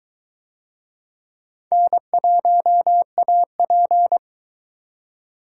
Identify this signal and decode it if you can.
Morse code, heard as a keyed tone, about 22 wpm, 700 Hz. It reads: N1AP